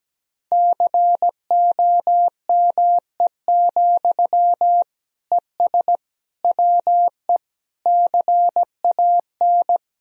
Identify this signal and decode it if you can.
Morse code, heard as a keyed tone, about 17 wpm, 700 Hz. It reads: COME, ES WE CAN
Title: Sentences from Top 500 Words and Abbreviations 17wpm